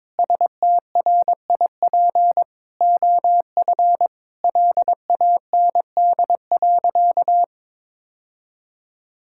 Morse code, heard as a keyed tone, about 23 wpm, 700 Hz. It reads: STRIP OF LAND.
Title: Sentences from Top 1600 Words 22wpm